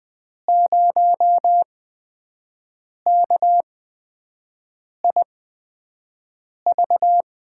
Morse code, heard as a keyed tone, about 20 wpm, spaced as for 5 wpm, 700 Hz. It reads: 0KIV
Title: USA Call Signs - 8x Character Spacing - 20wpm